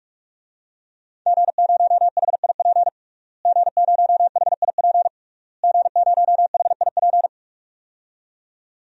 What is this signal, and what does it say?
Morse code, heard as a keyed tone, about 47 wpm, 700 Hz. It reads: G0HIP G0HIP G0HIP
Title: DXCC-100 Call Signs - 5 Characters - Repeated 3x - 45wpm